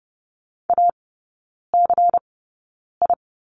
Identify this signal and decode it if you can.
Morse code, heard as a keyed tone, about 30 wpm, spaced as for 9 wpm, 700 Hz. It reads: ACI
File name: Lesson 17e - Letter u - Three Characters 30wpm